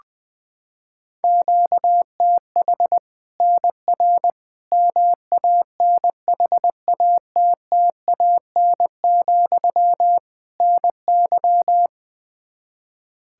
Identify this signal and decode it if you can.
Morse code, heard as a keyed tone, about 21 wpm, 700 Hz. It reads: QTH NR MANHATTAN, NY
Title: Practice QSO - Exchange 1 - Beginning - 20wpm